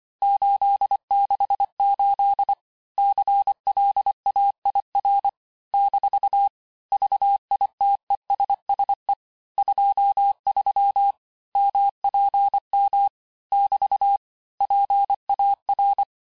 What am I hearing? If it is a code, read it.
Morse code, heard as a keyed tone, about 25 wpm, 800 Hz. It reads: 868 CLAIR - VITESSE 23 MPM <BT> PAR
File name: cours-868-teste-clair-vitesse-23mpm